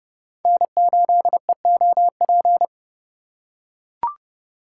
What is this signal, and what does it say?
Morse code, heard as a keyed tone, about 30 wpm, 700 Hz. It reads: N8EOP
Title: USA Call Signs - 5 Characters - 4x Character Spacing - 30wpm